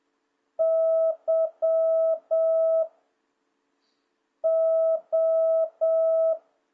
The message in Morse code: Y O

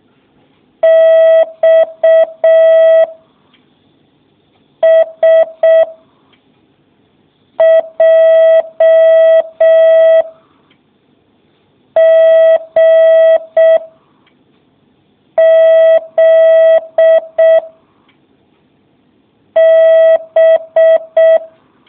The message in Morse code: X S J G Z B

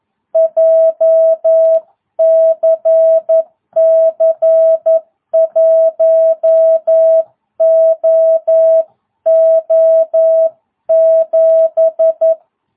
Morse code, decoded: JCC 1OO7

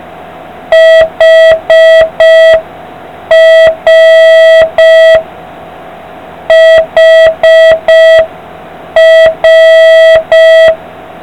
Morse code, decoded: HR HR

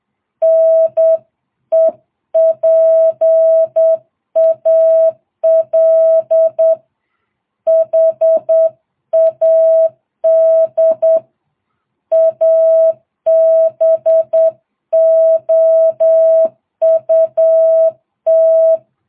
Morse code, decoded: NEPAL HAD ABOUT

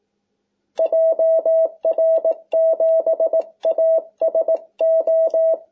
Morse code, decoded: JR7ASO

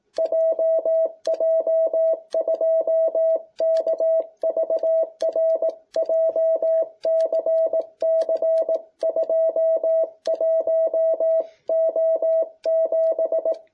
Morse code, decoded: JJ2KVR JCC 21O7

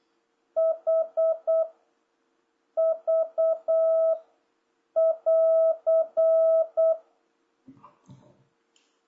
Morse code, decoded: H V <AR>

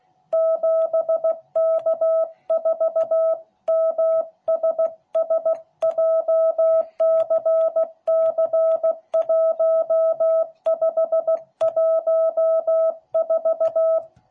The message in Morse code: 7K4MSS JCC1514

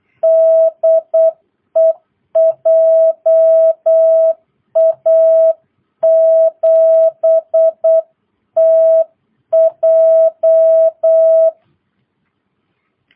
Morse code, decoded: DE JA7TJ

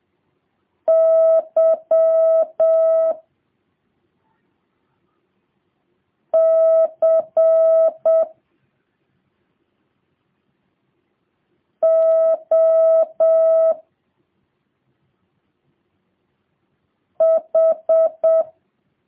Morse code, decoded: Y C O H

